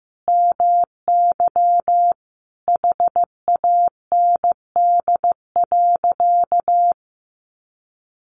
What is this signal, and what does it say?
Morse code, heard as a keyed tone, about 15 wpm, 700 Hz. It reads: MY HAND.
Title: Sentences from Top 1300 Words 15wpm